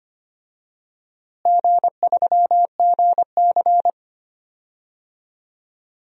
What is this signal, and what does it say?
Morse code, heard as a keyed tone, about 25 wpm, 700 Hz. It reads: G3GC
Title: DXCC-100 Call Signs - 4 Characters - Repeated 3x - 25wpm